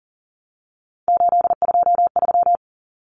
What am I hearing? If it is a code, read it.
Morse code, heard as a keyed tone, about 40 wpm, 700 Hz. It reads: 823
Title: Common Cut Numbers - 3-Digit 40wpm